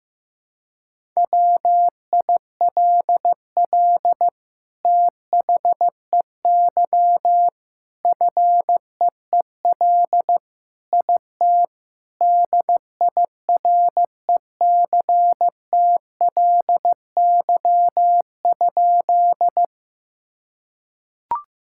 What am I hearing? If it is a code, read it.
Morse code, heard as a keyed tone, about 15 wpm, 700 Hz. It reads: WILL THEY FEEL IT DIRECTLY?
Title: Sentences from Top 1000 Words - Review Part 5 - 15wpm